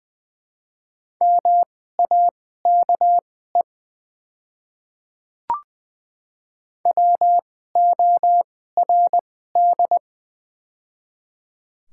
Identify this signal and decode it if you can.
Morse code, heard as a keyed tone, about 20 wpm, spaced as for 14 wpm, 700 Hz. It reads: MAKE WORD
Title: 2x Character Spacing - Top 100 Words - 20wpm